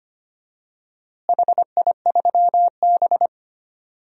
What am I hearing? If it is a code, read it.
Morse code, heard as a keyed tone, about 25 wpm, 700 Hz. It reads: HI3B
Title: DXCC-100 Call Signs - 4 Characters - Repeated 2x - 25wpm